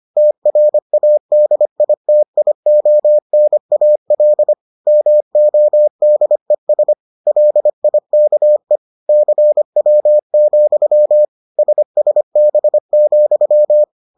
Morse code, heard as a keyed tone, about 25 wpm, 600 Hz. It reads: TRADITIONAL MODES LIKE CW, SSB,